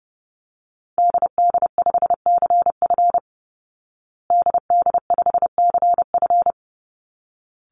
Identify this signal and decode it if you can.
Morse code, heard as a keyed tone, about 28 wpm, 700 Hz. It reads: DD5CF DD5CF
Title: Call Signs Repeated 3x - 30wpm